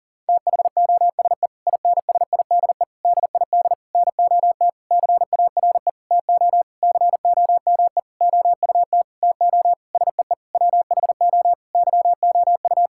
Morse code, heard as a keyed tone, about 40 wpm, 700 Hz. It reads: THOSE INSIDE DID NOT CARE TO COME OUT TO SEE WHO YOU
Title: Sentences from Top 1000 Words - Review Part 6 - 40wpm